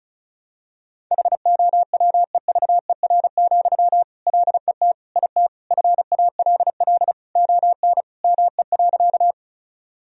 Morse code, heard as a keyed tone, about 35 wpm, 700 Hz. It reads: HOWEVER, LET IT FALL ON ME.
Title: Sentences from Top 1000 Words - Review Part 2 - 35wpm